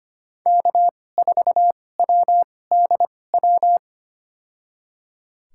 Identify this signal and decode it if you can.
Morse code, heard as a keyed tone, about 25 wpm, spaced as for 18 wpm, 700 Hz. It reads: K4WDW
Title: USA Call Signs - 5 Characters - 2x Character Spacing - 25wpm